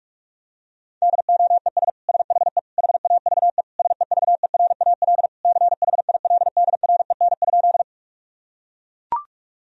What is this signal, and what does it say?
Morse code, heard as a keyed tone, about 41 wpm, 700 Hz. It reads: DOES SHE HAVE SEVERAL CHILDREN?
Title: Sentences from Top 1000 Words - Review Part 2 - 45wpm